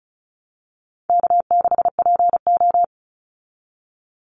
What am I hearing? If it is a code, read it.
Morse code, heard as a keyed tone, about 34 wpm, 700 Hz. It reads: K6PO
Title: Easy Call Signs Repeated 2x - 35wpm